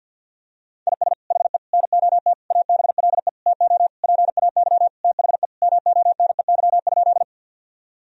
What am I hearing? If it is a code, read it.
Morse code, heard as a keyed tone, about 50 wpm, 700 Hz. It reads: IS HE NOT ABLE TO PAY THE MONEY?
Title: Sentences from Top 1000 Words - Review Part 2 - 50wpm